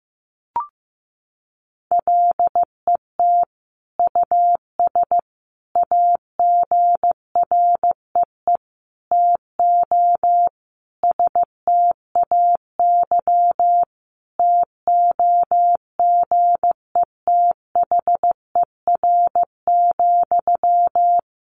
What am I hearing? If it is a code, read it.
Morse code, heard as a keyed tone, about 15 wpm, 700 Hz. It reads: LET US AGREE TO STAY TOGETHER,
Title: Sentences from Top 1100 Words 15wpm